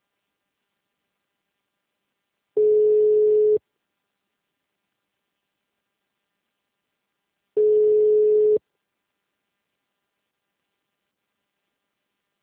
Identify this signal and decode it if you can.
Ringback tone (Europe, 425 Hz).
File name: out- -2002- - - .66433